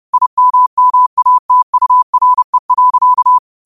Morse code, heard as a keyed tone, about 30 wpm, 1000 Hz. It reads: IMMATURE.
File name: qotdfiles / 30wpm